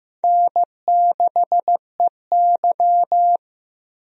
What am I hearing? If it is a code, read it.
Morse code, heard as a keyed tone, about 15 wpm, 700 Hz. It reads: N6EY
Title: USA Call Signs - 4 Characters - 15wpm